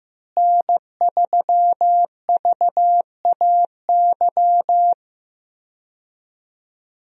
Morse code, heard as a keyed tone, about 16 wpm, 700 Hz. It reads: N3VAY
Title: USA Call Signs - 5 Characters - Repeated 2x - 15wpm